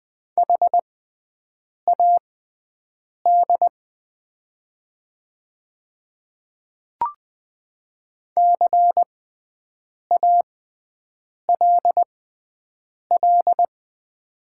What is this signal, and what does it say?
Morse code, heard as a keyed tone, about 20 wpm, spaced as for 7 wpm, 690 Hz. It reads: HAD CALL
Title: 6x Character Spacing - Top 500 Words - 20wpm